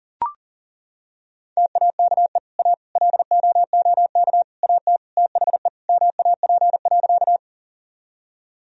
Morse code, heard as a keyed tone, about 40 wpm, 700 Hz. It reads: TAKE A LOOK AT THE MAP.
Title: Sentences from Top 1000 Words - Review Part 2 - 40wpm